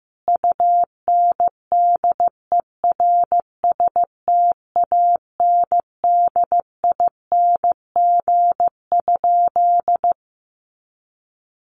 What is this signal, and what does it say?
Morse code, heard as a keyed tone, about 15 wpm, 700 Hz. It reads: UNDERSTANDING?
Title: Sentences from Top 1500 Words 15wpm